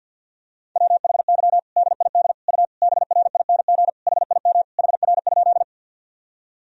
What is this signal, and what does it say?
Morse code, heard as a keyed tone, about 48 wpm, 700 Hz. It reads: WHY DID U BRING HIM HR?
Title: Sentences from Top 500 Words and Abbreviations 50wpm